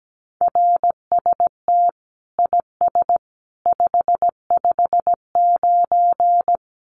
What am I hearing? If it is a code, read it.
Morse code, heard as a keyed tone, about 17 wpm, 700 Hz. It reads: RST IS 559
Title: Common QSO Phrases 17wpm